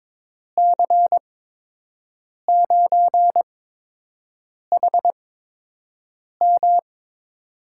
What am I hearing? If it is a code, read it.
Morse code, heard as a keyed tone, about 22 wpm, 700 Hz. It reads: C 9 H M
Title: USA Call Signs - 5 Characters - 8x Character Spacing - 22wpm